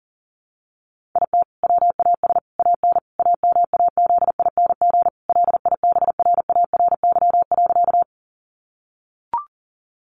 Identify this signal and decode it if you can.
Morse code, heard as a keyed tone, about 37 wpm, 700 Hz. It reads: IT WAS AN AMAZING LIBRARY.
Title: Sentences from Top 1500 Words 40wpm